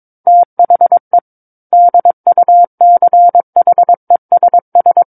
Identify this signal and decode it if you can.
Morse code, heard as a keyed tone, about 22 wpm, 700 Hz. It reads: THE DUCHESS
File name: 06-Chapter-06 22 WPM